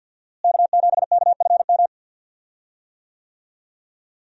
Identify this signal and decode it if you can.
Morse code, heard as a keyed tone, about 46 wpm, 700 Hz. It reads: K7KFM
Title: USA Call Signs - 5 Characters - Repeated 2x - 50wpm